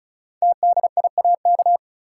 Morse code, heard as a keyed tone, about 36 wpm, 700 Hz. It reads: TDIAK
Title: Code Groups - 5 Letters - 35wpm